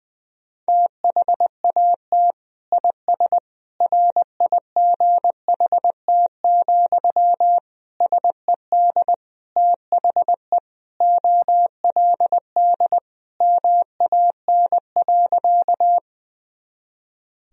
Morse code, heard as a keyed tone, about 20 wpm, 700 Hz. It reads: THAT IS RIGHT, SED THE OLD MAN.
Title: Sentences from Top 200 Words and Abbreviations 20wpm